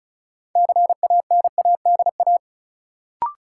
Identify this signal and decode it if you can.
Morse code, heard as a keyed tone, about 36 wpm, 700 Hz. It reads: CANADA
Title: Sets of 5 Words - New Word Drill - 35wpm